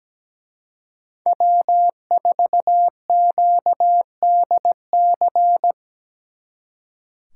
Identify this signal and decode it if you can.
Morse code, heard as a keyed tone, about 20 wpm, 700 Hz. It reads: W4QDC
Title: Speed Racing - USA 1x3 Call Signs 17wpm